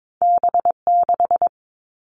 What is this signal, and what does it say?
Morse code, heard as a keyed tone, about 22 wpm, 700 Hz. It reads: B6
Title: All Cut Numbers - 2-Digit 22wpm